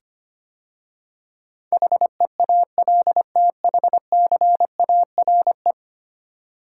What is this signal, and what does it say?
Morse code, heard as a keyed tone, about 25 wpm, 700 Hz. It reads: HEALTHCARE
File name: 10 Letter Words - First 3 Letters Given - 25wpm